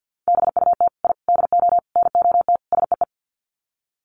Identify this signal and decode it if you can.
Morse code, heard as a keyed tone, about 50 wpm, 700 Hz. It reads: BUT I DO NOT SEE
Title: Sets of 5 Words Encore 50wpm